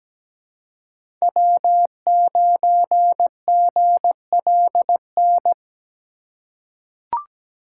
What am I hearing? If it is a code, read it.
Morse code, heard as a keyed tone, about 17 wpm, 700 Hz. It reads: W9GLN
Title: USA Call Signs - 5 Characters - 4x Character Spacing - 17wpm